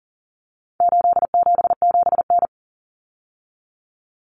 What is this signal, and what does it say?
Morse code, heard as a keyed tone, about 37 wpm, 700 Hz. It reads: 877N
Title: Common Cut Numbers - 4-Digit 40wpm